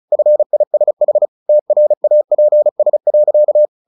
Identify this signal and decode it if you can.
Morse code, heard as a keyed tone, about 34 wpm, 600 Hz. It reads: FISH TRAPS.
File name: the old man and the sea 2 35wpm